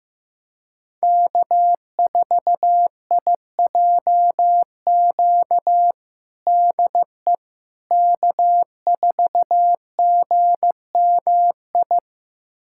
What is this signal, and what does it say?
Morse code, heard as a keyed tone, about 15 wpm, 700 Hz. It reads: K4IJQ DE K4GMI
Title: USA Callsign DE Callsign 15wpm